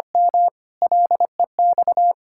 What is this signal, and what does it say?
Morse code, heard as a keyed tone, about 25 wpm, 700 Hz. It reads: M LEX
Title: Pangram Challenge v2 - 25wpm